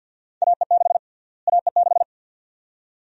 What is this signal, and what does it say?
Morse code, heard as a keyed tone, about 52 wpm, 700 Hz. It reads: AE6 AE6
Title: USA Call Sign Prefixes - Repeated 2x - 50wpm